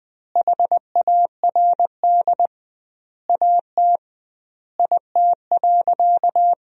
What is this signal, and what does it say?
Morse code, heard as a keyed tone, about 21 wpm, 700 Hz. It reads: HARD AT IT.
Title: 2x Word-Spacing - Sentences from Top 300 Words - 20wpm